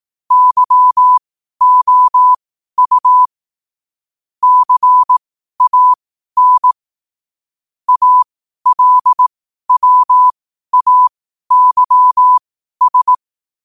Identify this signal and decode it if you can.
Morse code, heard as a keyed tone, about 18 wpm, spaced as for 12 wpm, 1000 Hz. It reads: YOU CAN ALWAYS